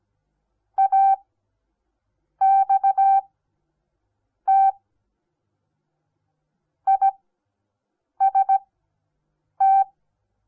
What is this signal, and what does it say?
Morse code, heard as a keyed tone, about 17 wpm, spaced as for 6 wpm, 780 Hz. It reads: AXT IST
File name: Onlinelehrgang / morsen / DJ4UF Morsen08